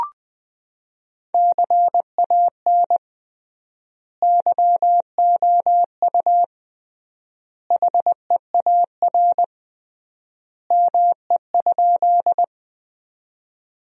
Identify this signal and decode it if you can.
Morse code, heard as a keyed tone, about 20 wpm, 700 Hz. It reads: CAN YOU HEAR ME?
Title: 3x Word-Spacing - Sentences from Top 300 Words - 20wpm